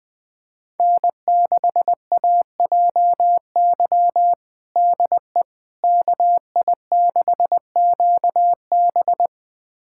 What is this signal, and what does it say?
Morse code, heard as a keyed tone, about 20 wpm, 700 Hz. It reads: N6AJY DE KI6QB
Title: USA Callsign DE Callsign - 5 Characters - 20wpm